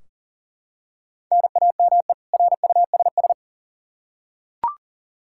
Morse code, heard as a keyed tone, about 40 wpm, 700 Hz. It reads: NAME RUSS
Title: K1USN SST QSO Practice 40wpm